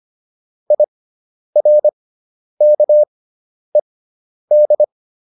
Morse code, heard as a keyed tone, about 25 wpm, spaced as for 10 wpm, 600 Hz. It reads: IRKED